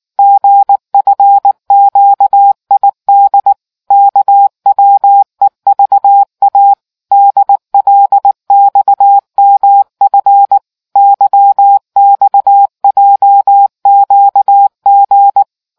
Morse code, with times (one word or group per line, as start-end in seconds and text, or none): 0.2-3.5 GFQID
3.9-6.7 KWEVA
7.1-10.6 DLXMF
11.0-15.4 YXJQG